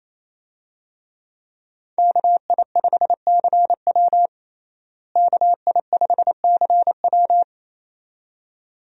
Morse code, heard as a keyed tone, about 28 wpm, 700 Hz. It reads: KI5CW KI5CW
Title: USA Call Signs - 5 Characters - Repeated 2x - 28wpm